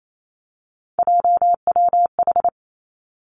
Morse code, heard as a keyed tone, about 28 wpm, 700 Hz. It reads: JWH
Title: Three Challenging Letter-Number Combinations 28wpm